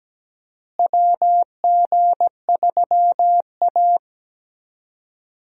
Morse code, heard as a keyed tone, about 17 wpm, 700 Hz. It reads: WG3A
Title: USA Call Signs - 4 Characters - Repeated 3x - 17wpm